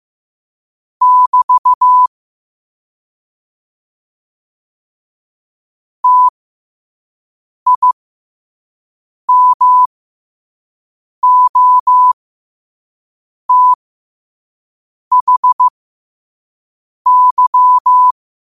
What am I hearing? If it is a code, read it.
Morse code, heard as a keyed tone, about 15 wpm, spaced as for 5 wpm, 1000 Hz. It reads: <BT> TIMOTHY
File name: qotdfiles / 5wpm